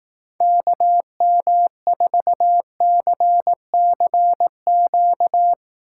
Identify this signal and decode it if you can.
Morse code, heard as a keyed tone, about 18 wpm, 700 Hz. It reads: KM4CCQ
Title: Speed Racing - USA 2x3 Call Signs 15wpm